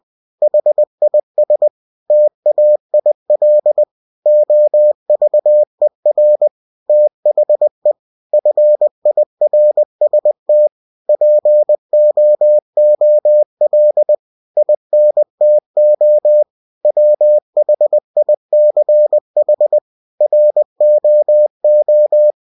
Morse code, heard as a keyed tone, about 20 wpm, 600 Hz. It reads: HIS TAIL OVER THE FIRST POOL INTO WHICH ROO
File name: winnie-20wpm